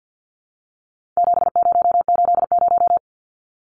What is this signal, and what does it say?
Morse code, heard as a keyed tone, about 52 wpm, 700 Hz. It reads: 7080
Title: Numbers - 4-Digit 50wpm